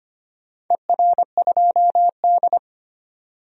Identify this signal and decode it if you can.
Morse code, heard as a keyed tone, about 25 wpm, 700 Hz. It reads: ER2D
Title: DXCC-100 Call Signs - 4 Characters - 25wpm